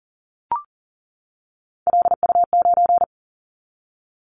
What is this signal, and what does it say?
Morse code, heard as a keyed tone, about 42 wpm, 700 Hz. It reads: LU9
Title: Callsign Prefixes for Top 50 DXCC - 40wpm